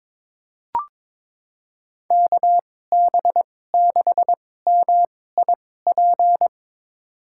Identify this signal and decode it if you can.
Morse code, heard as a keyed tone, about 22 wpm, spaced as for 16 wpm, 700 Hz. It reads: KB6MIP
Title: USA Call Signs - 6 Characters - 2x Character Spacing - 22wpm